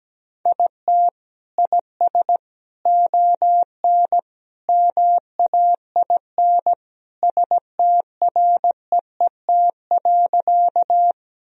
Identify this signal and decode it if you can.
Morse code, heard as a keyed tone, about 17 wpm, 700 Hz. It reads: IT IS ON MAIN STREET.
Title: Sentences from Top 1000 Words - Review Part 2 - 17wpm